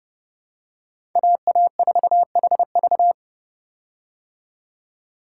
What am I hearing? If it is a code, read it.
Morse code, heard as a keyed tone, about 32 wpm, 700 Hz. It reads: AA4HV
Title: K1USN SST QSO Practice - 30wpm